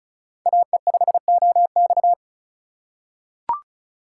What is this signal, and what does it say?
Morse code, heard as a keyed tone, about 36 wpm, 700 Hz. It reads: AE5OX
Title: USA Call Signs - 5 Characters - 3x Character Spacing - 35wpm